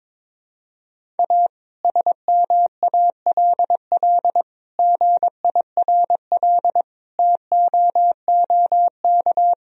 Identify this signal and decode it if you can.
Morse code, heard as a keyed tone, about 22 wpm, 700 Hz. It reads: A SMALL GIRL TOOK
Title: Sentences from Top 1000 Words - Review Part 1 - 22wpm